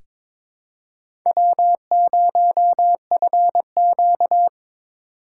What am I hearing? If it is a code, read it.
Morse code, heard as a keyed tone, about 22 wpm, 700 Hz. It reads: W0FQ
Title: USA Call Signs - 8x Character Spacing - 22wpm